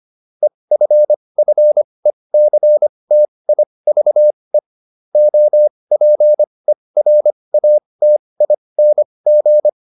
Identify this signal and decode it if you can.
Morse code, heard as a keyed tone, about 25 wpm, spaced as for 20 wpm, 600 Hz. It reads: EFFECTIVE OPERATING